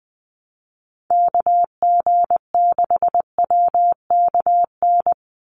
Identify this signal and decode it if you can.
Morse code, heard as a keyed tone, about 20 wpm, 700 Hz. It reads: KG6WKN
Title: USA Call Signs 20wpm